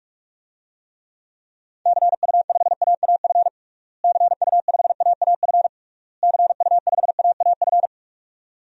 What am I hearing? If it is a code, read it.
Morse code, heard as a keyed tone, about 46 wpm, 700 Hz. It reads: CU5AAF CU5AAF CU5AAF
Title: DXCC-100 Call Signs - 6 Characters - Repeated 3x - 45wpm